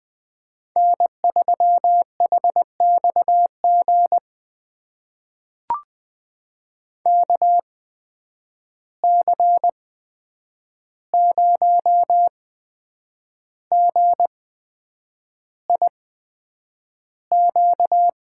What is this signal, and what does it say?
Morse code, heard as a keyed tone, about 20 wpm, spaced as for 8 wpm, 700 Hz. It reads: N3HXG KC0GIQ
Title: USA Call Signs - 8x Character Spacing - 20wpm